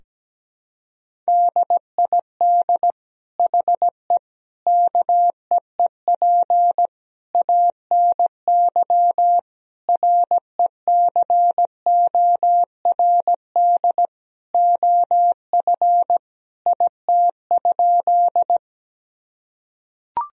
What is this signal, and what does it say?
Morse code, heard as a keyed tone, about 17 wpm, 700 Hz. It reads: DID HE KEEP ANY RECORD OF IT?
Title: Sentences from Top 1000 Words - Review Part 3 - 17wpm